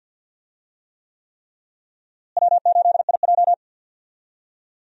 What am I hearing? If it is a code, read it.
Morse code, heard as a keyed tone, about 52 wpm, 700 Hz. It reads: W8IJ
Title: USA Call Signs - 4 Characters - Repeated 3x - 50wpm